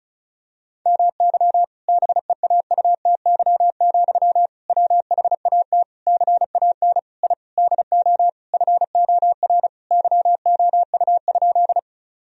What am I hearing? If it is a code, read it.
Morse code, heard as a keyed tone, about 34 wpm, 700 Hz. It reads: MY BEAUTY, WHAT CAN I DO FOR YOU?
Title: Sentences from Top 1000 Words - Review Part 2 - 35wpm